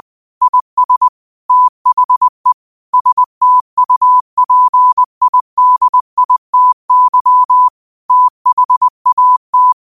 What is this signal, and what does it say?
Morse code, heard as a keyed tone, about 20 wpm, 1000 Hz. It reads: IS THE STUPIDITY THAT
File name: qotdfiles / 20wpm